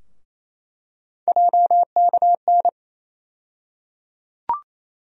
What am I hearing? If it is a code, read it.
Morse code, heard as a keyed tone, about 28 wpm, 700 Hz. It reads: JKN
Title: USA Call Sign Postfixes - 3 Letters 28wpm